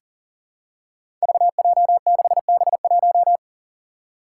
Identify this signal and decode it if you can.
Morse code, heard as a keyed tone, about 39 wpm, 700 Hz. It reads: VJ6B1
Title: Five Challenging Letter-Number Combinations 40wpm